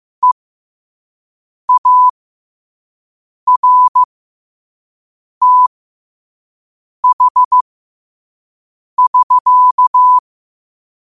Morse code, heard as a keyed tone, about 15 wpm, spaced as for 5 wpm, 1000 Hz. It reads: EARTH<SK>
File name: qotdfiles / 5wpm